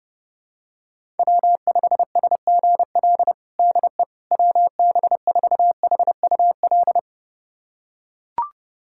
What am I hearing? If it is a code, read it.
Morse code, heard as a keyed tone, about 30 wpm, 700 Hz. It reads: W5SGL DE WB4HUL
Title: USA Callsign DE Callsign 30wpm